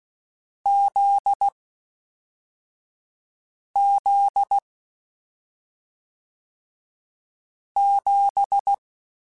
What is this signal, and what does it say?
Morse code, heard as a keyed tone, about 16 wpm, 800 Hz. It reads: Z Z 7
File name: Difficult Combos / Difficult Combos-mp3s / awj1 ndb6 gz7 @5